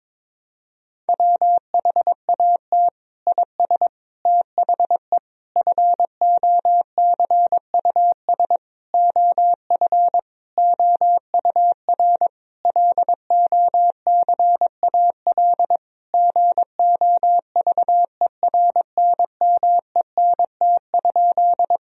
Morse code, heard as a keyed tone, about 22 wpm, 700 Hz. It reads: WHAT IS THE FOCUS OF OUR LOCAL GOVERNMENT?
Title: Sentences from Top 1000 Words - Review Part 4 - 22wpm